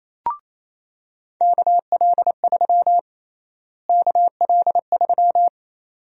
Morse code, heard as a keyed tone, about 28 wpm, 700 Hz. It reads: KL3 KL3
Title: USA Call Sign Prefixes - Repeated 2x - 28wpm